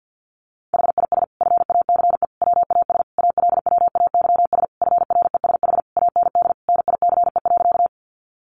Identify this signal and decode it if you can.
Morse code, heard as a keyed tone, about 50 wpm, 700 Hz. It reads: HIS FACE WAS ALWAYS FRESH AND NICE.
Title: Sentences from Top 1200 Words 50wpm